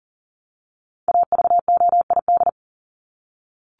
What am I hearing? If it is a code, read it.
Morse code, heard as a keyed tone, about 40 wpm, 700 Hz. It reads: AVOID
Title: Top 751-1000 Words 40wpm